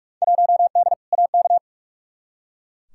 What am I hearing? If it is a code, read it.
Morse code, heard as a keyed tone, about 42 wpm, 700 Hz. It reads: 1D AK
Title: Field Day Exchanges - 45wpm